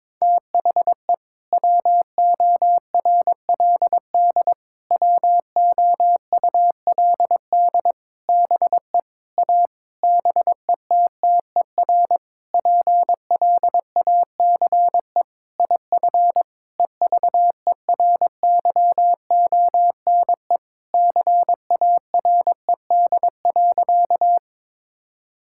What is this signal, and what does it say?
Morse code, heard as a keyed tone, about 22 wpm, 700 Hz. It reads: THE WORLD WOULD BE A BETTER PLACE IF EVERYONE CARED.
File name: practice / 22wpm / Sentences with 8-Word Phrases - 22wpm